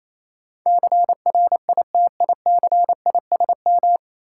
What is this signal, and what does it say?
Morse code, heard as a keyed tone, about 28 wpm, 700 Hz. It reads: CRITICISM
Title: Sentences from Top 1300 Words - New Word Drill - 28wpm